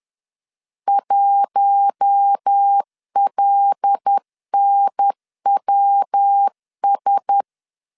Morse code, heard as a keyed tone, about 10 wpm, 800 Hz. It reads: 1LNWS
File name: LEZIONI / 6 PKX0 / 6 PKX0 10wpm 2707